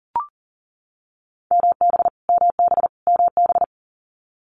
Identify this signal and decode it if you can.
Morse code, heard as a keyed tone, about 42 wpm, 690 Hz. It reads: MB MB MB
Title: Multi-letter Phonemes Repeated 3x 40wpm